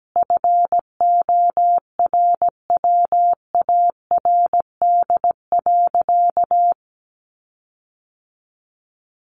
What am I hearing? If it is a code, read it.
Morse code, heard as a keyed tone, about 17 wpm, 700 Hz. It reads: FORWARD.
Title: Sentences from Top 1300 Words 17wpm